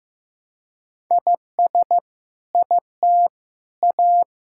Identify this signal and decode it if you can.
Morse code, heard as a keyed tone, about 15 wpm, 700 Hz. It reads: IS IT A
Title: Sentences from Top 1000 Words - Review Part 6 - 15wpm